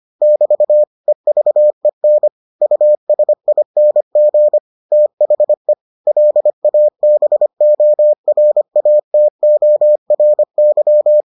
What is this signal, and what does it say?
Morse code, heard as a keyed tone, about 25 wpm, 600 Hz. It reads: <BT> EVEN USING THE LABORATORY